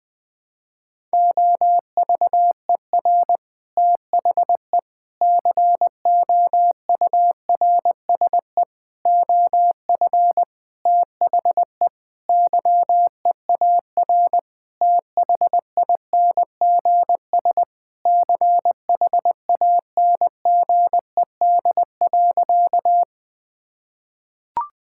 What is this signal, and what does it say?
Morse code, heard as a keyed tone, about 21 wpm, 700 Hz. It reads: OVER THE COURSE OF THE YEAR THINGS CHANGED.
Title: Sentences with 5-Word Phrases - 20wpm